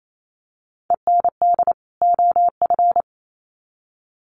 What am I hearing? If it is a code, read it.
Morse code, heard as a keyed tone, about 28 wpm, 700 Hz. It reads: END OF
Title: Sets of 2 Words 28wpm